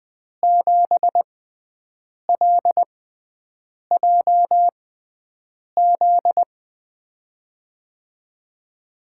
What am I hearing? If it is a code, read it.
Morse code, heard as a keyed tone, about 20 wpm, spaced as for 7 wpm, 700 Hz. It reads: 7LJZ
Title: USA Call Signs - 5 Characters - 6x Character Spacing - 20wpm